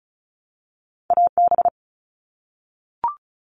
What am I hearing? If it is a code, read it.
Morse code, heard as a keyed tone, about 36 wpm, 700 Hz. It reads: AB
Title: Canadian Provinces and Territories - Repeated 3x - 35wpm